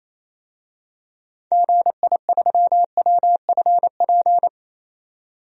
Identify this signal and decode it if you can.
Morse code, heard as a keyed tone, about 28 wpm, 700 Hz. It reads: GI3WFP
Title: DXCC-100 Call Signs - 6 Characters - 28wpm